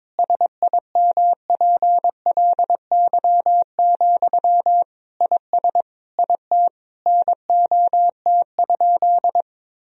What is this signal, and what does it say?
Morse code, heard as a keyed tone, about 22 wpm, 700 Hz. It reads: SIMPLY, IS IT NOT?
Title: Sentences from Top 1000 Words - Review Part 4 - 22wpm